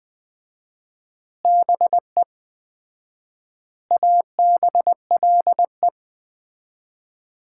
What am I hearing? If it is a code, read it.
Morse code, heard as a keyed tone, about 20 wpm, 700 Hz. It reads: BE ABLE
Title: 4x Word-Spacing - Sets of 4 Words - 20wpm